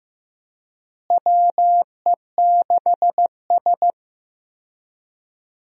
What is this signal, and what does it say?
Morse code, heard as a keyed tone, about 15 wpm, 700 Hz. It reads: WE6S
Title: USA Call Signs - 4 Characters - Repeated 3x - 15wpm